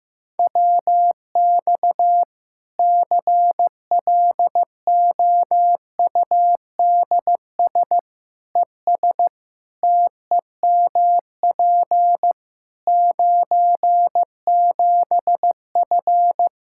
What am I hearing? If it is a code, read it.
Morse code, heard as a keyed tone, about 16 wpm, 700 Hz. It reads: WX CLOUDS ES TEMP 97F
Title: Practice QSO - Exchange 2 - Info - 15wpm